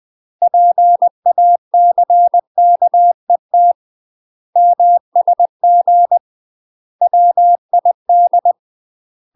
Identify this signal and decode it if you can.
Morse code, heard as a keyed tone, about 20 wpm, 700 Hz. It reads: PACKET MSG WID
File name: UK MASTER QSOs TRAINING / 20WPM / UK MASTER QSOs 0072